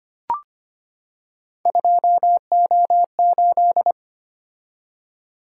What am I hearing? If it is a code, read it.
Morse code, heard as a keyed tone, about 25 wpm, 700 Hz. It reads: 2O8
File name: Three Letter-Number Combinations - 25wpm